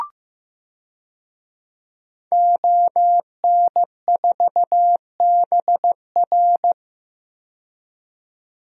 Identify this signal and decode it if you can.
Morse code, heard as a keyed tone, about 15 wpm, 700 Hz. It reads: ON4BR
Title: DXCC-100 Call Signs - 5 Characters - 15wpm